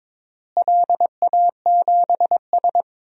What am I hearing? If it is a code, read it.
Morse code, heard as a keyed tone, about 22 wpm, 700 Hz. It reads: LA7S
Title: DXCC-100 Call Signs - 4 Characters - Repeated 3x - 22wpm